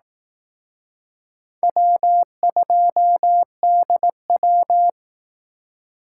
Morse code, heard as a keyed tone, about 18 wpm, 700 Hz. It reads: W2DW
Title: Speed Racing - USA 1x2 Call Signs 15wpm